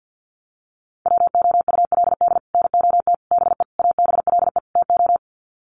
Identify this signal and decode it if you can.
Morse code, heard as a keyed tone, about 50 wpm, 700 Hz. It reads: WOULD NOT BE ABLE TO
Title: Sets of 5 Words 50wpm